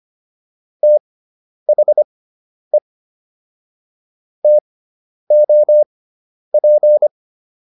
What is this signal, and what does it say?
Morse code, heard as a keyed tone, about 25 wpm, spaced as for 10 wpm, 600 Hz. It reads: THE TOP